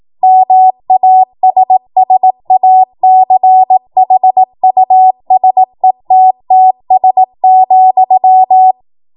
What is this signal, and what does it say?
Morse code, heard as a keyed tone, about 18 wpm, 750 Hz. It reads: MASSACHUSETTS,
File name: Morse / 18WPM